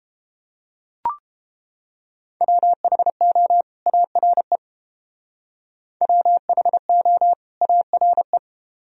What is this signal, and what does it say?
Morse code, heard as a keyed tone, about 31 wpm, 700 Hz. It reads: WHO ARE WHO ARE